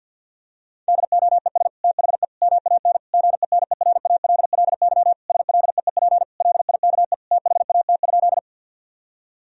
Morse code, heard as a keyed tone, about 50 wpm, 700 Hz. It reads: DOES THE MAN GENERALLY SLEEP LIKE THAT?
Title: Sentences from Top 1000 Words - Review Part 4 - 50wpm